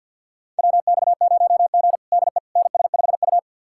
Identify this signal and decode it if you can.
Morse code, heard as a keyed tone, about 51 wpm, 700 Hz. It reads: WX0G DE NS5U
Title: USA Callsign DE Callsign - 4 Characters - 50wpm